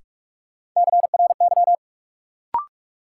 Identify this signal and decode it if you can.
Morse code, heard as a keyed tone, about 41 wpm, 700 Hz. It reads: CRY
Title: Sentences from Top 500 Words - New Word Drill - 45wpm